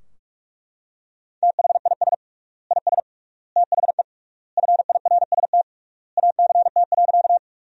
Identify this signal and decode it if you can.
Morse code, heard as a keyed tone, about 41 wpm, 700 Hz. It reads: THIS IS THE FIRST ACT.
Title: 3x Word-Spacing - Sentences from Top 200 Words - 45wpm